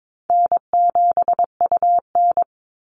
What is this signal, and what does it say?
Morse code, heard as a keyed tone, about 22 wpm, 700 Hz. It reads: N7UN
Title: NA SOTA Chaser Callsigns 22wpm